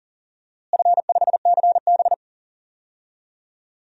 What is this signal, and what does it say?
Morse code, heard as a keyed tone, about 39 wpm, 700 Hz. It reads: F5CB
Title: DXCC-100 Call Signs - 4 Characters - Repeated 2x - 40wpm